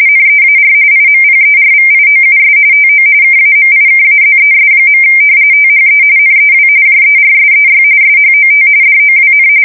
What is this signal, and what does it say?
RTTY, about 45.45 baud. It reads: NEAR IMPERVIOUS TO KAMIKAZES. THE NEW JERSEY SHOT DOWN
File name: fsk170 / battleshipnj